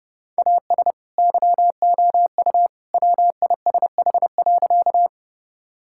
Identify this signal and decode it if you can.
Morse code, heard as a keyed tone, about 32 wpm, 700 Hz. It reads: AS YOU WISH.
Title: Sentences from Top 1600 Words 30wpm